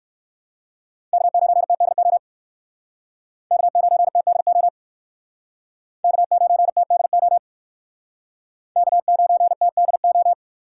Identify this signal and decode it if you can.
Morse code, heard as a keyed tone, about 66 wpm, 700 Hz. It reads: K9TDO K9TDO K9TDO K9TDO